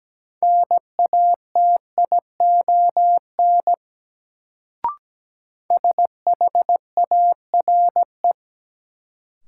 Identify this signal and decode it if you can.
Morse code, heard as a keyed tone, about 17 wpm, 700 Hz. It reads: NATION SHARE
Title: Sentences from Top 900 Words - New Word Drill - 17wpm